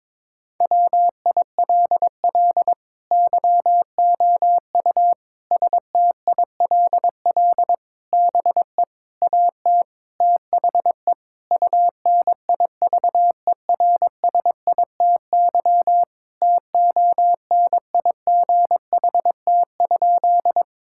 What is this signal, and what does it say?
Morse code, heard as a keyed tone, about 22 wpm, 700 Hz. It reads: WILL YOU STILL BE AT THE UNIVERSITY TONIGHT?
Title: Sentences from Top 1000 Words - Review Part 5 - 22wpm